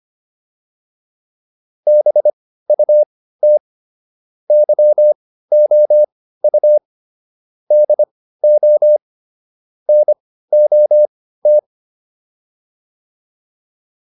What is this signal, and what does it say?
Morse code, heard as a keyed tone, about 25 wpm, spaced as for 15 wpm, 600 Hz. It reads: BUT YOU DO NOT
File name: WD305 15